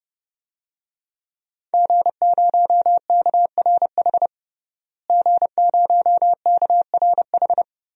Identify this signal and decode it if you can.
Morse code, heard as a keyed tone, about 30 wpm, 700 Hz. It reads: G0KRH G0KRH
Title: DXCC-100 Call Signs - 5 Characters - Repeated 2x - 30wpm